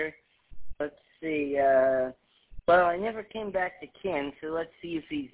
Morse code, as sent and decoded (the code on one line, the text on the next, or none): none